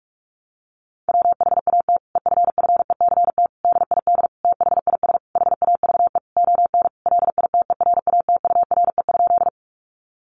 Morse code, heard as a keyed tone, about 44 wpm, 700 Hz. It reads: WHAT EFFECT DID THIS HAVE ON LITERATURE?
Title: Sentences from Top 1000 Words 45wpm